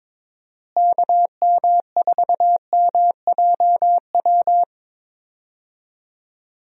Morse code, heard as a keyed tone, about 22 wpm, 700 Hz. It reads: KM4MJW
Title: USA Call Signs - 6 Characters - Repeated 3x - 22wpm